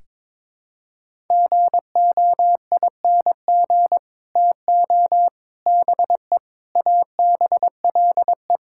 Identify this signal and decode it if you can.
Morse code, heard as a keyed tone, about 22 wpm, 700 Hz. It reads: GOING TO BE ABLE
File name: practice / 22wpm / Build A Phrase - 5-Word Phrases - 22wpm